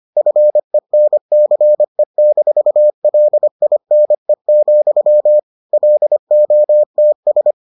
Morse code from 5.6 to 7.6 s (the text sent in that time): LOTS